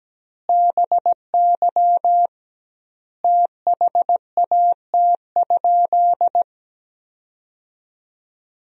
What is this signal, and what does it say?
Morse code, heard as a keyed tone, about 17 wpm, 700 Hz. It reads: BY THAT?
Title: 2x Word-Spacing - Sentences from Top 200 Words - 17wpm